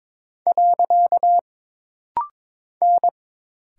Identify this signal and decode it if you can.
Morse code, heard as a keyed tone, about 22 wpm, 700 Hz. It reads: . N
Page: This rapid-fire timing is geared to push your ICR (Instant Character Recognition) skills.